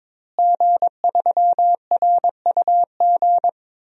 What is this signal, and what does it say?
Morse code, heard as a keyed tone, about 22 wpm, 700 Hz. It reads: G3RUG
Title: DXCC-100 Call Signs - 5 Characters - Repeated 2x - 22wpm